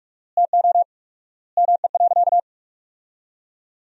Morse code, heard as a keyed tone, about 46 wpm, 700 Hz. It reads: TO ME.
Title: 4x Word-Spacing - Sentences from Top 200 Words - 45wpm